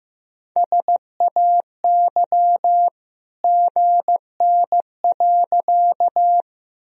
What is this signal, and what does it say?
Morse code, heard as a keyed tone, about 15 wpm, 700 Hz. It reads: SAY GN.
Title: Sentences from Top 300 Words and Abbreviations 15wpm